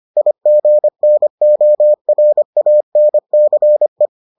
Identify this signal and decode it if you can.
Morse code, heard as a keyed tone, about 25 wpm, 600 Hz. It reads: IGNORANCE